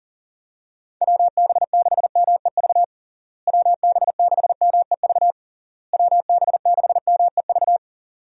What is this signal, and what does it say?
Morse code, heard as a keyed tone, about 40 wpm, 700 Hz. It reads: WB6MEV WB6MEV WB6MEV
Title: USA Call Signs - 6 Characters - Repeated 3x - 40wpm